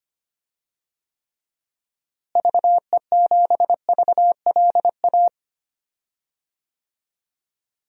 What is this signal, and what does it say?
Morse code, heard as a keyed tone, about 25 wpm, 700 Hz. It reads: VE7VLA
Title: DXCC-100 Call Signs - 6 Characters - Repeated 2x - 25wpm